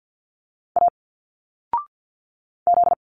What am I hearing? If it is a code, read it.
Morse code, heard as a keyed tone, about 47 wpm, 700 Hz. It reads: A Z
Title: Single Letters 50wpm